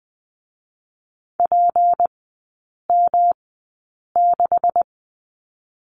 Morse code, heard as a keyed tone, about 20 wpm, 700 Hz. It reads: P M 6
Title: 198 Lesson 32e - Letter k - Three Characters 20wpm